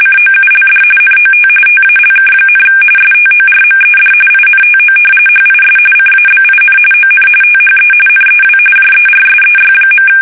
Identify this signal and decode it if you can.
RTTY, about 45.45 baud. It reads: COULD NOT TALK, STORMED INTO HIS CABIN ON THE NEW JERSEY.
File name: fsk850 / battleshipnj